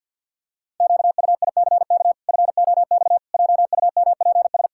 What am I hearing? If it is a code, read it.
Morse code, heard as a keyed tone, about 50 wpm, 700 Hz. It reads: QUICK FOX JUMPS
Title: Pangram Challenge v2 - 50wpm